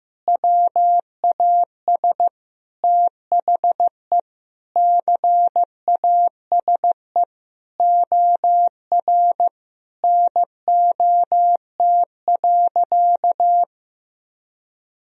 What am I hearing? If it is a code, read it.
Morse code, heard as a keyed tone, about 15 wpm, 700 Hz. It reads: WAS THE CASE OR NOT.
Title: Sentences from Top 1000 Words - Review Part 3 - 15wpm